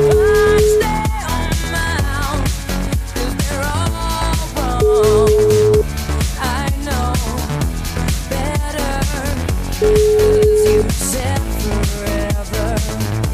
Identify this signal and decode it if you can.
Ringback tone (Europe, 425 Hz), with noise.